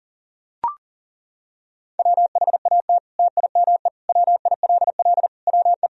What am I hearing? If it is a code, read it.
Morse code, heard as a keyed tone, about 40 wpm, 700 Hz. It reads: WHAT TIME WILL WE
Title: Sentences from Top 1000 Words - Review Part 1 - 40wpm